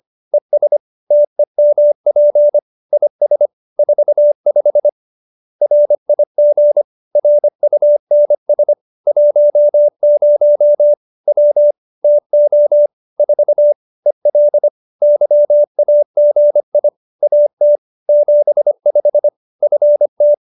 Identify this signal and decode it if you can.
Morse code, heard as a keyed tone, about 25 wpm, 590 Hz. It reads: ES TEMP IS 45 RIG RUNS 10 W TO 4 EL YAGI AT 75 FT